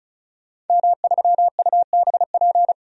Morse code, heard as a keyed tone, about 34 wpm, 700 Hz. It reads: M3UBP
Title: DXCC-100 Call Signs - 5 Characters - 35wpm